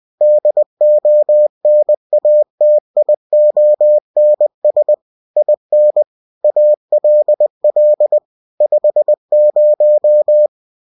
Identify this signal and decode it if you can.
Morse code, heard as a keyed tone, about 20 wpm, 600 Hz. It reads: DONATIONS IN ALL 50
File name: cw-books / winnie-the-pooh / winnie-20wpm-page-0121